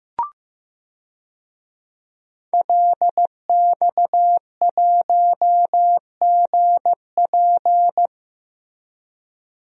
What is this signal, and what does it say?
Morse code, heard as a keyed tone, about 15 wpm, 700 Hz. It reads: LX1GP
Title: DXCC-100 Call Signs - 5 Characters - Repeated 2x - 15wpm